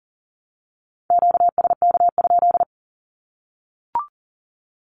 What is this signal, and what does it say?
Morse code, heard as a keyed tone, about 40 wpm, 700 Hz. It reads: QSK?
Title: Q-Codes Repeated 3x 40wpm